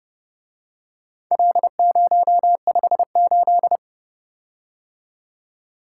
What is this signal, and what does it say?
Morse code, heard as a keyed tone, about 29 wpm, 700 Hz. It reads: L058